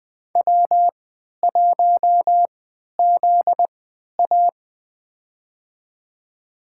Morse code, heard as a keyed tone, about 21 wpm, spaced as for 11 wpm, 700 Hz. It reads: W1ZA
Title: USA Call Signs - 3x Character Spacing - 20wpm